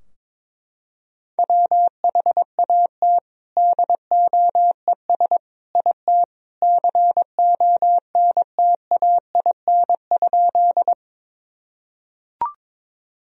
Morse code, heard as a keyed tone, about 22 wpm, 700 Hz. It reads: WHAT DOES IT CONTAIN?
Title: Sentences from Top 1000 Words - Review Part 2 - 22wpm